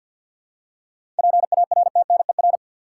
Morse code, heard as a keyed tone, about 50 wpm, 700 Hz. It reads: PARTNER
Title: Seven Letter Words - 50wpm